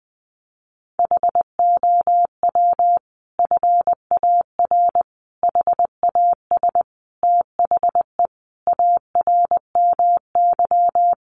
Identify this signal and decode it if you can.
Morse code, heard as a keyed tone, about 19 wpm, 700 Hz. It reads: HOW FAR HAS THE ARMY
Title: Sentences from Top 1200 Words 20wpm